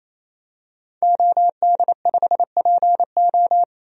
Morse code, heard as a keyed tone, about 28 wpm, 700 Hz. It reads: OD5PO
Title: DXCC-100 Call Signs - 5 Characters - 28wpm